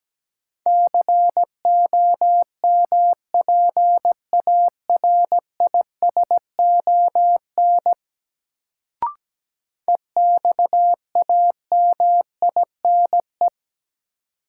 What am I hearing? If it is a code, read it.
Morse code, heard as a keyed tone, about 17 wpm, 700 Hz. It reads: COMPARISON EXAMINE
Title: Sentences from Top 1600 Words - New Word Drill Alt - 17wpm